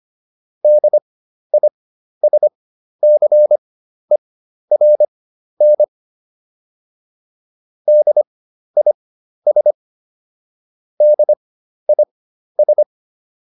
Morse code, heard as a keyed tone, about 25 wpm, spaced as for 12 wpm, 600 Hz. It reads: DISCERN DIS DIS